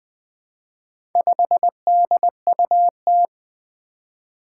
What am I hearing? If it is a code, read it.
Morse code, heard as a keyed tone, about 21 wpm, 700 Hz. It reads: 5DUT
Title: Four Challenging Letter-Number Combinations 20wpm